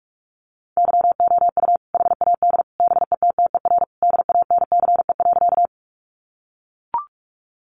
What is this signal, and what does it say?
Morse code, heard as a keyed tone, about 44 wpm, 700 Hz. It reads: YOU HAD BETTER DANCE.
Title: Sentences from Top 700 Words 45wpm